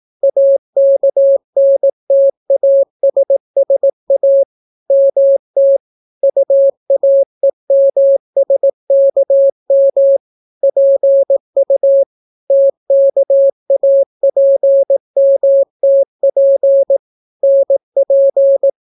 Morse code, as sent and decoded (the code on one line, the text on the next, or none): .- -.- -. - .- ... ... .- / -- - / ..- .- . -- ... -.- -- / .--. ..- / - -.- .- .--. -- - .--. / -. .--.
AKNTASSA MT UAEMSKM PU TKAPMTP NP